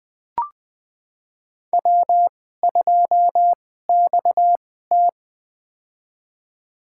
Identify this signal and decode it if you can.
Morse code, heard as a keyed tone, about 20 wpm, spaced as for 15 wpm, 700 Hz. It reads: W2XT
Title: USA Call Signs - 4 Characters - 2x Character Spacing - 20wpm